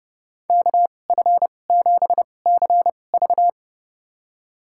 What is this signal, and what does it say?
Morse code, heard as a keyed tone, about 30 wpm, spaced as for 22 wpm, 700 Hz. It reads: KF7CV
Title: USA Call Signs - 2x Character Spacing - 30wpm